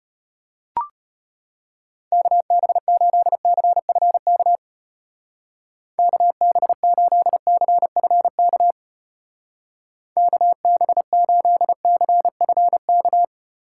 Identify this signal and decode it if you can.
Morse code, heard as a keyed tone, about 32 wpm, 700 Hz. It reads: KB8CFK KB8CFK KB8CFK